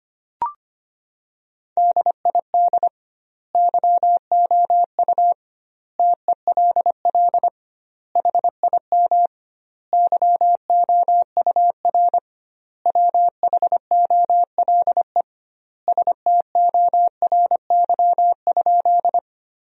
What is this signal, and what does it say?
Morse code, heard as a keyed tone, about 25 wpm, 700 Hz. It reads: DID YOU TELL HIM YOUR WHOLE STORY?
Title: 2x Word-Spacing - Sentences from Top 400 Words - 25wpm